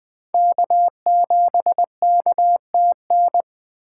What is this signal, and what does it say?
Morse code, heard as a keyed tone, about 20 wpm, 700 Hz. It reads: K7KTN
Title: Speed Racing - USA Call Signs 17wpm